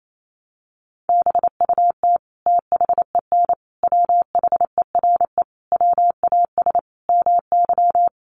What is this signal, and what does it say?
Morse code, heard as a keyed tone, about 28 wpm, 700 Hz. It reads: BUT THEN WHERE WAS MY
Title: Sentences from Top 1300 Words 28wpm